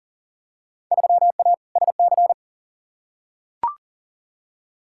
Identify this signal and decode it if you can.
Morse code, heard as a keyed tone, about 40 wpm, 700 Hz. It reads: 3A SC
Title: Field Day Exchanges Encore - 40wpm